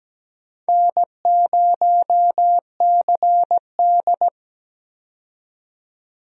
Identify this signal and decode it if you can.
Morse code, heard as a keyed tone, about 17 wpm, 700 Hz. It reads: N0CD
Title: USA Call Signs - 4 Characters - 17wpm